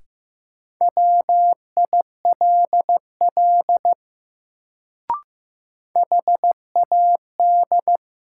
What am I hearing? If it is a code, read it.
Morse code, heard as a keyed tone, about 15 wpm, 700 Hz. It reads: WILL HAD
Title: Sentences from Top 100 Words - New Word Drill - 15wpm